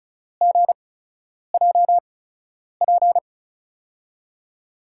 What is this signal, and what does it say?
Morse code, heard as a keyed tone, about 36 wpm, spaced as for 10 wpm, 700 Hz. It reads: GJP
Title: USA Call Signs - 5 Characters - 8x Character Spacing - 35wpm